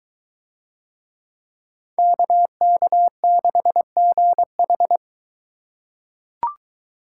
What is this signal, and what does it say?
Morse code, heard as a keyed tone, about 23 wpm, 700 Hz. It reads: KK6GH